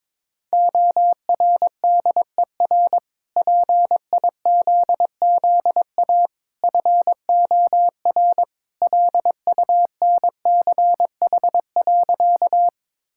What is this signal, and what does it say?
Morse code, heard as a keyed tone, about 22 wpm, 700 Hz. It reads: ORDER PIZZA FOR LUNCH.
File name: practice / 22wpm / Easy Sentences with Hints - 22wpm